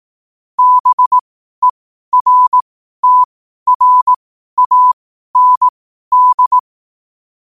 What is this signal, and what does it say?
Morse code, heard as a keyed tone, about 18 wpm, spaced as for 12 wpm, 1000 Hz. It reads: BERTRAND
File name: qotdfiles / 12wpm